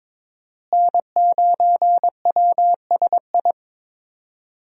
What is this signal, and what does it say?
Morse code, heard as a keyed tone, about 22 wpm, 700 Hz. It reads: N9WSI
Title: USA Call Signs - 5 Characters - Repeated 3x - 22wpm